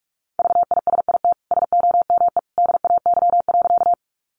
Morse code, heard as a keyed tone, about 44 wpm, 700 Hz. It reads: VISIT SOME DAY.
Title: Sentences from Top 900 Words 45wpm